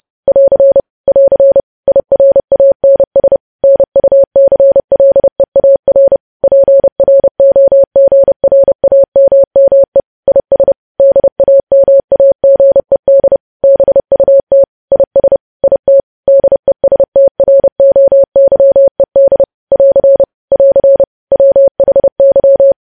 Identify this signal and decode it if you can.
Morse code, heard as a keyed tone, about 30 wpm, 550 Hz. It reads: <AR> <AR> IRANS NUCLEAR PROGRAMME IS DAMAGED BUT IS IT DESTROYED <AR> <AR> WHY
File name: podcast / News headlines-1x-30x30